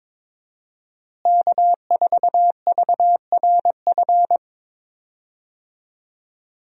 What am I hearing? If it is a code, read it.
Morse code, heard as a keyed tone, about 22 wpm, 700 Hz. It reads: K4VRF
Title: USA Call Signs - 5 Characters - Repeated 2x - 22wpm